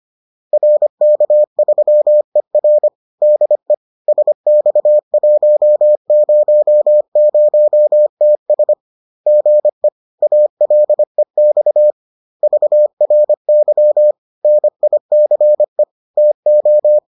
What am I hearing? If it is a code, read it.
Morse code, heard as a keyed tone, about 25 wpm, 600 Hz. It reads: RK3ER DE SX100TS GE ALEX VRY NICE TO